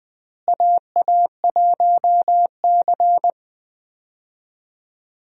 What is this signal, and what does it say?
Morse code, heard as a keyed tone, about 20 wpm, 700 Hz. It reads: AA1C
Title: USA Call Signs - 4 Characters - Repeated 2x - 20wpm